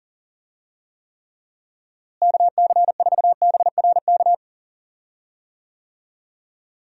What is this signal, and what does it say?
Morse code, heard as a keyed tone, about 38 wpm, 700 Hz. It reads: KC4BRK
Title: USA Call Signs - 6 Characters - Repeated 2x - 40wpm